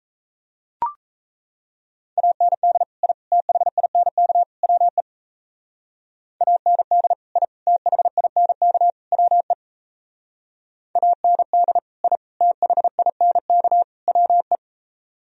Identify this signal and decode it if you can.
Morse code, heard as a keyed tone, about 36 wpm, 700 Hz. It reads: AND I THINK WE AND I THINK WE AND I THINK WE